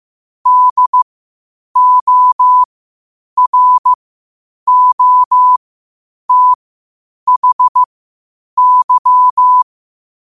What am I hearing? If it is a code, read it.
Morse code, heard as a keyed tone, about 15 wpm, spaced as for 8 wpm, 1000 Hz. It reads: DOROTHY
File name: qotdfiles / 8wpm